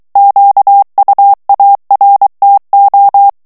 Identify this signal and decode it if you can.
Morse code, heard as a keyed tone, about 23 wpm, 800 Hz. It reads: QUARTO
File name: ITALIEN texte clair 2 vitesse 22 mots minute